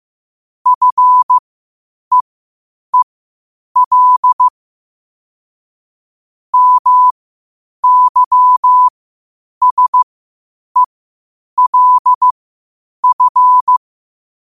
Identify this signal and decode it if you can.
Morse code, heard as a keyed tone, about 15 wpm, spaced as for 8 wpm, 1000 Hz. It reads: FEEL MYSELF
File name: qotdfiles / 8wpm